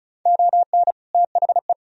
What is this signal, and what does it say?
Morse code, heard as a keyed tone, about 36 wpm, 700 Hz. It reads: ON THE
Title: Sentences from Top 1000 Words - Review Part 2 - 35wpm